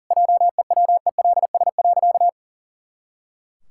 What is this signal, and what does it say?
Morse code, heard as a keyed tone, about 37 wpm, 700 Hz. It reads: JEWELS.
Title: Pangram Challenge v2 - 40wpm